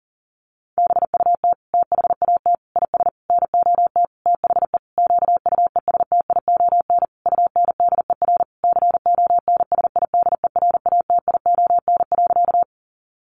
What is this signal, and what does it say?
Morse code, heard as a keyed tone, about 40 wpm, 700 Hz. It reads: BUT THAT IS NOT THE QUESTION UNDER CONSIDERATION.
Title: Sentences from Top 1500 Words 40wpm